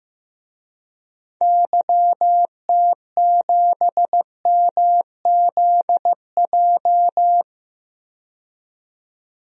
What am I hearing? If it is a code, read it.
Morse code, heard as a keyed tone, about 15 wpm, 700 Hz. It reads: YT7MZJ
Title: DXCC-100 Call Signs - 6 Characters - 15wpm